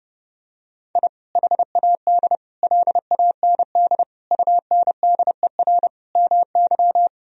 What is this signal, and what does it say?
Morse code, heard as a keyed tone, about 30 wpm, 700 Hz. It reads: I HAD LAND UNDER MY
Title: Sentences from Top 1000 Words - Review Part 1 - 30wpm